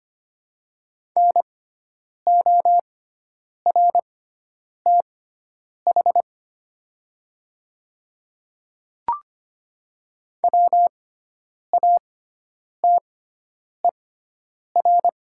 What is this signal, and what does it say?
Morse code, heard as a keyed tone, about 25 wpm, spaced as for 9 wpm, 700 Hz. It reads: NORTH WATER
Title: 6x Character Spacing - Top 500 Words - 25wpm